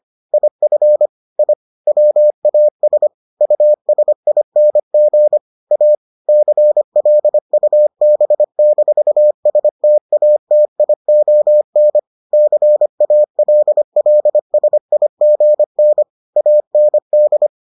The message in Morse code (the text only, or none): IF I WAS USING A CLUB-STATION CALLSIGN AND